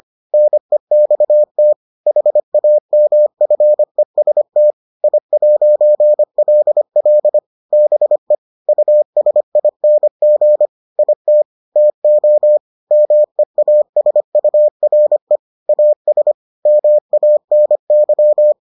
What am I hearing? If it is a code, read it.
Morse code, heard as a keyed tone, about 25 wpm, 600 Hz. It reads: NEXT HAMFEST I'LL BE USING IT TO MEASURE AS MANY